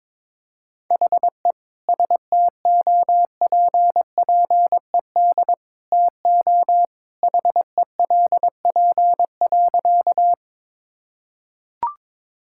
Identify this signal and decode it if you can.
Morse code, heard as a keyed tone, about 22 wpm, 700 Hz. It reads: HE STOPPED TO HELP.
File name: practice / 22wpm / Sentences with 2-Word Phrases - 22wpm